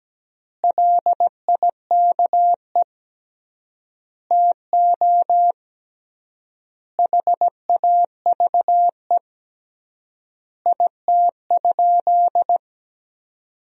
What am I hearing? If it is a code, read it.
Morse code, heard as a keyed tone, about 17 wpm, 700 Hz. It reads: LIKE TO HAVE IT?
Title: 3x Word-Spacing - Sentences from Top 100 Words - 17wpm